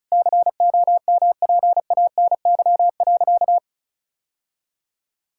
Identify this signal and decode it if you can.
Morse code, heard as a keyed tone, about 34 wpm, 700 Hz. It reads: COMPANY.
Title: Sentences from Top 1000 Words - Review Part 5 - 35wpm